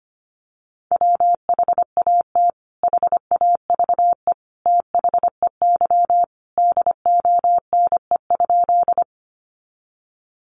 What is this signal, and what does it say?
Morse code, heard as a keyed tone, about 25 wpm, 700 Hz. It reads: WHAT HAVE THEY DONE?
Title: Sentences from Top 500 Words 25wpm